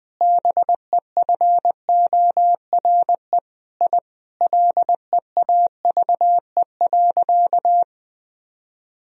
Morse code, heard as a keyed tone, about 20 wpm, 700 Hz. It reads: BEFORE I LEAVE.
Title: Sentences from Top 1000 Words - Review Part 1 - 20wpm